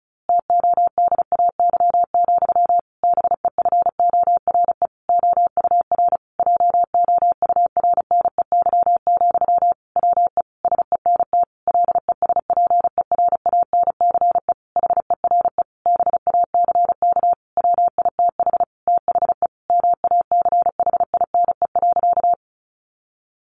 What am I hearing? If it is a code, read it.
Morse code, heard as a keyed tone, about 34 wpm, 700 Hz. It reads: TODAY, BEFORE OUR JOURNEY, WE SENT LESPERANCE HERE BACK WITH THE MACHINE.
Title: A Sound Of Thunder 35wpm